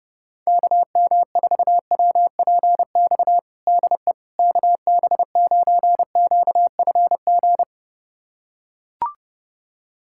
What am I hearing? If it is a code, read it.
Morse code, heard as a keyed tone, about 30 wpm, 700 Hz. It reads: KM4WPX DE KB9QFG
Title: USA Callsign DE Callsign - 6 Characters - 30wpm